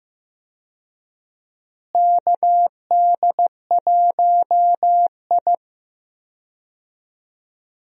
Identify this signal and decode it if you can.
Morse code, heard as a keyed tone, about 15 wpm, 700 Hz. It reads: KD1I
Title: USA Call Signs - 4 Characters - Repeated 3x - 15wpm